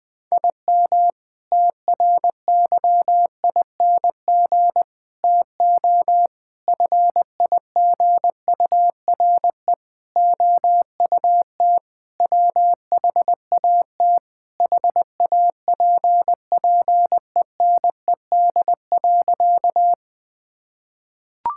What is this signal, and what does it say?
Morse code, heard as a keyed tone, about 20 wpm, 700 Hz. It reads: IM TRYING TO FIGURE OUT WHAT HAPPENED.
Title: Sentences with 5-Word Phrases - 20wpm